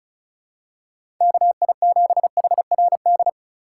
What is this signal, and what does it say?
Morse code, heard as a keyed tone, about 36 wpm, 700 Hz. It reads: KI7HRD
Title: USA Call Signs - 6 Characters - 4x Character Spacing - 35wpm